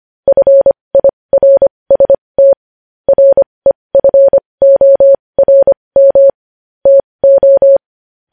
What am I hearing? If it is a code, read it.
Morse code, heard as a keyed tone, about 25 wpm, spaced as for 20 wpm, 550 Hz. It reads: FIRST REFORM TO
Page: news-headlines-in-morse-code-at-20-wpm